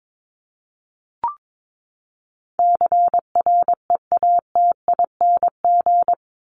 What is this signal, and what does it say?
Morse code, heard as a keyed tone, about 22 wpm, 700 Hz. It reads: CREATING
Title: ING Suffix - Top 100 Words with Base Words 22wpm